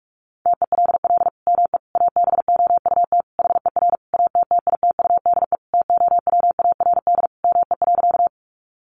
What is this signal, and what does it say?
Morse code, heard as a keyed tone, about 44 wpm, 700 Hz. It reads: TELL ME ABOUT HER ATTITUDE TOWARD ME.
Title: Sentences from Top 1300 Words 45wpm